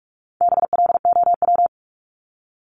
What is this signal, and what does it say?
Morse code, heard as a keyed tone, about 44 wpm, 700 Hz. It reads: BLOW
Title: Top 1751-2000 Words 45wpm